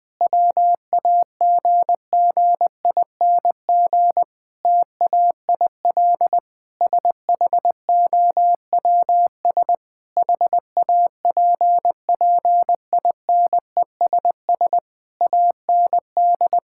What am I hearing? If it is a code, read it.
Morse code, heard as a keyed tone, about 20 wpm, 700 Hz. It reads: WAGGING TAIL SHOWS HAPPINESS AND
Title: Difficult Sentences with Hints - 20wpm